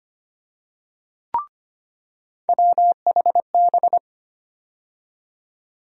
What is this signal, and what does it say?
Morse code, heard as a keyed tone, about 25 wpm, 700 Hz. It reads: WHB
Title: USA Call Sign Postfixes - 3 Letters 25wpm